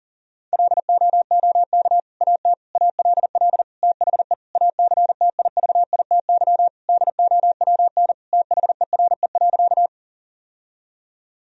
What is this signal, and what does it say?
Morse code, heard as a keyed tone, about 40 wpm, 700 Hz. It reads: LOOK AT ALL THE ACTIVITY DOWN THERE.
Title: Sentences from Top 1000 Words - Review Part 4 - 40wpm